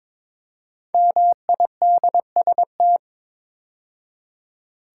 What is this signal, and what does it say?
Morse code, heard as a keyed tone, about 22 wpm, 690 Hz. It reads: MIDST
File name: 5 Letter Words - First 3 Letters Given - 22wpm